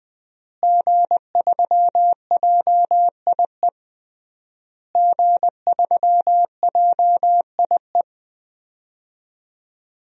Morse code, heard as a keyed tone, about 21 wpm, 700 Hz. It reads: G3JIE G3JIE
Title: DXCC-100 Call Signs - 5 Characters - Repeated 3x - 20wpm